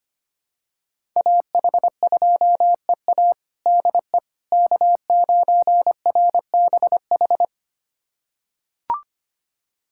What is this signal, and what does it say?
Morse code, heard as a keyed tone, about 25 wpm, 700 Hz. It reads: AH2EA DE K9RBH
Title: USA Callsign DE Callsign - 5 Characters - 25wpm